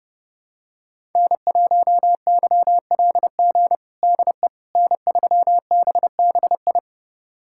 Morse code, heard as a keyed tone, about 31 wpm, 700 Hz. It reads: N1YLG DE N3BBI
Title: USA Callsign DE Callsign - 5 Characters - 30wpm